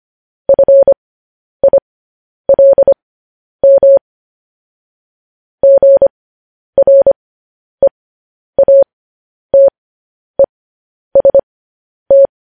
Morse code, heard as a keyed tone, about 25 wpm, spaced as for 10 wpm, 550 Hz. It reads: FILM GREATEST
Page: News Headlines in Morse Code at 10 WPM